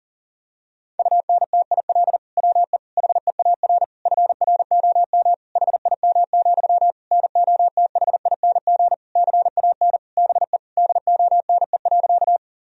Morse code, heard as a keyed tone, about 40 wpm, 700 Hz. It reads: UNTIL WE HEAR FROM HIM, NOTHING CAN BE DONE.
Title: Sentences from Top 1000 Words - Review Part 1 - 40wpm